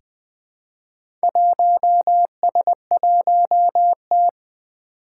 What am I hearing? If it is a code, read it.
Morse code, heard as a keyed tone, about 20 wpm, 700 Hz. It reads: 1S1T
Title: Four Challenging Letter-Number Combinations 20wpm